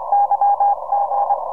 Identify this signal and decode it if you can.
Morse code, heard as a keyed tone, about 25 wpm, 810 Hz, with noise. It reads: YG